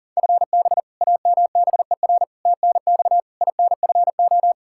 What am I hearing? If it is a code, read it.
Morse code, heard as a keyed tone, about 40 wpm, 700 Hz. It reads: FB AMBER TNX INFO
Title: Practice QSO - Exchange 3 - More Info - 40wpm